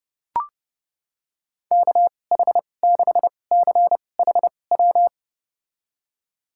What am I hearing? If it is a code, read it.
Morse code, heard as a keyed tone, about 30 wpm, spaced as for 22 wpm, 700 Hz. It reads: KH6CHW
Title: USA Call Signs - 6 Characters - 2x Character Spacing - 30wpm